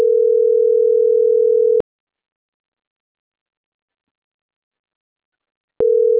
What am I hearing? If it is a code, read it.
Ringback tone (North America), 440 + 480 Hz.